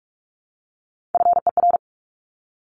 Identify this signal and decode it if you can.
Morse code, heard as a keyed tone, about 44 wpm, 700 Hz. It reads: FER
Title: Abbreviations 45wpm